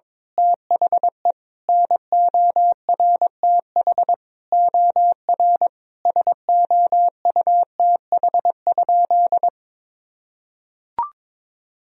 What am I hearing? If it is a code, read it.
Morse code, heard as a keyed tone, about 22 wpm, 700 Hz. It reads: THE NORTH OR SOUTH?
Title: Sentences from Top 1000 Words - Review Part 2 - 22wpm